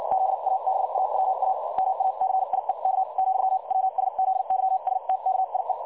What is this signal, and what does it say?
Morse code, heard as a keyed tone, about 56 wpm, spaced as for 47 wpm, 760 Hz. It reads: ZABAWA MACIEK , WSZYSTKIE
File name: dlazaawansowanych / nagraniazpasma / websdr recording 3565.0kHz